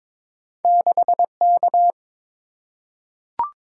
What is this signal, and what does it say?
Morse code, heard as a keyed tone, about 22 wpm, 700 Hz. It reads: 6K
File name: practice / 22wpm / Two Letter-Number Combinations - 22wpm